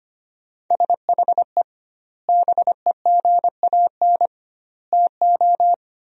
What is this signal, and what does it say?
Morse code, heard as a keyed tone, about 25 wpm, 700 Hz. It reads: SHE BEGAN TO
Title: 2x Word-Spacing - Sentences from Top 500 Words - 25wpm